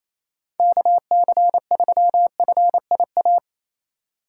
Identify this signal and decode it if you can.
Morse code, heard as a keyed tone, about 28 wpm, 700 Hz. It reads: KC3FIA
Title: USA Call Signs - 6 Characters - 3x Character Spacing - 28wpm